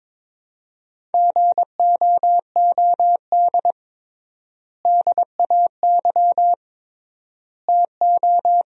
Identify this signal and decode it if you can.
Morse code, heard as a keyed tone, about 22 wpm, 700 Hz. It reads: GOOD DAY TO
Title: 3x Word-Spacing - Sentences from Top 200 Words - 22wpm